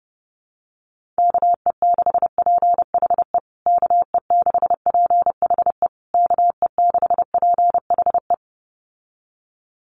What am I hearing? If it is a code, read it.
Morse code, heard as a keyed tone, about 30 wpm, 700 Hz. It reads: KE6PHE KE6PHE KE6PHE
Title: USA Call Signs Repeated 3x 30wpm